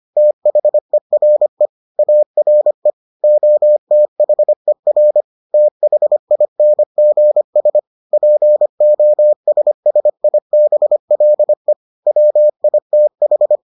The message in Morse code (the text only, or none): THERE ARE OTHER THINGS POSSIBLE WITH